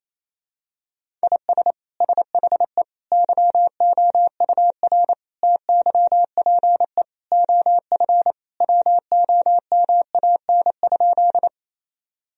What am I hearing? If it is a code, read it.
Morse code, heard as a keyed tone, about 28 wpm, 700 Hz. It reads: IS SHE YOUR TYPE OF WOMAN?
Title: Sentences from Top 1000 Words - Review Part 3 - 28wpm